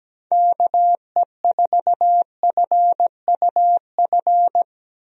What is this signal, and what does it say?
Morse code, heard as a keyed tone, about 17 wpm, 700 Hz. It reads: KE4FUF
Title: USA Call Signs - 6 Characters - Repeated 2x - 17wpm